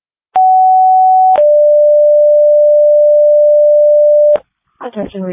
Two-tone page, heard as a one-second tone, then a three-second tone, about 750 then 590 Hz. It is heard from 0.4 to 4.4 s.